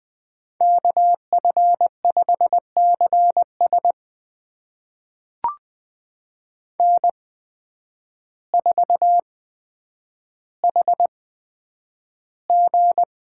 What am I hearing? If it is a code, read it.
Morse code, heard as a keyed tone, about 20 wpm, spaced as for 9 wpm, 700 Hz. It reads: KF5CS N4HG
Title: USA Call Signs - 8x Character Spacing - 20wpm